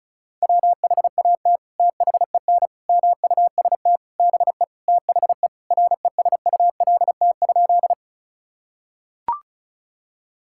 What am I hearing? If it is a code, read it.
Morse code, heard as a keyed tone, about 36 wpm, 700 Hz. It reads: WHAT THEN MUST BE THE RESULT?
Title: Sentences from Top 1000 Words - Review Part 5 - 35wpm